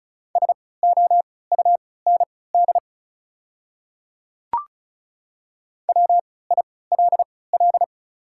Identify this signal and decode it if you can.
Morse code, heard as a keyed tone, about 36 wpm, spaced as for 20 wpm, 690 Hz. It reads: SOUND WILL
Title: 3x Character Spacing - Top 100 Words - 35wpm